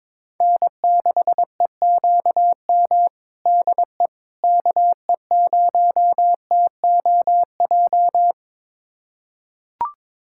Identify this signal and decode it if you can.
Morse code, heard as a keyed tone, about 22 wpm, 700 Hz. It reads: N6EQM DE KE0TOJ
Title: USA Callsign DE Callsign 22wpm